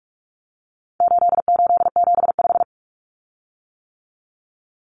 Morse code, heard as a keyed tone, about 44 wpm, 700 Hz. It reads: 8875
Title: Common Cut Numbers - 4-Digit 45wpm